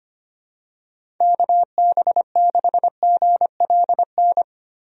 Morse code, heard as a keyed tone, about 25 wpm, 700 Hz. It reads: KB6GLN
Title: USA Call Signs - 6 Characters - Repeated 3x - 25wpm